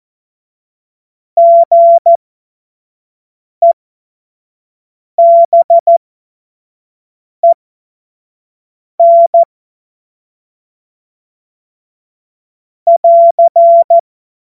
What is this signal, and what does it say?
Morse code, heard as a keyed tone, about 14 wpm, spaced as for 5 wpm, 680 Hz. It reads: GEBEN <AR>
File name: Morsekurs / Witze / Witz 07. September 2021, 25 F70